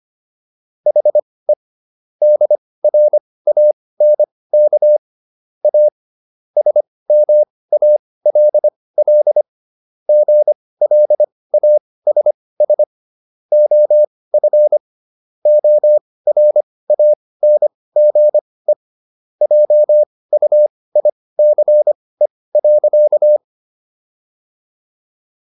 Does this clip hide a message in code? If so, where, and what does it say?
Yes: Morse code from 0.9 to 23.4 s, heard as a keyed tone, about 25 wpm, spaced as for 18 wpm, 600 Hz. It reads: HE DRANK A SMALL GLASS OF ORANGE JUICE.